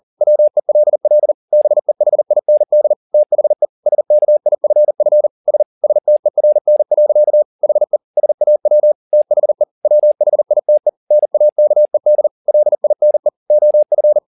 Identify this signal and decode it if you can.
Morse code, heard as a keyed tone, about 40 wpm, 600 Hz. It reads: WELL BEHIND THE SKIFF S STERN. HE SAW THE WHITE NAKED LINE OF
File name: the old man and the sea 2 40wpm